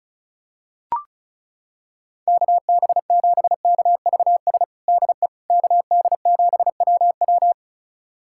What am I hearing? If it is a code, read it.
Morse code, heard as a keyed tone, about 34 wpm, 700 Hz. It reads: KB7KVS DE KD7WW
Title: USA Callsign DE Callsign 35wpm